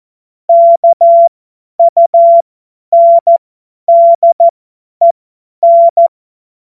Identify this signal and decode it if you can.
Morse code, heard as a keyed tone, about 14 wpm, spaced as for 10 wpm, 680 Hz. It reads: KUNDEN
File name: Morsekurs / Witze / Witz 14.November 2023 50 F70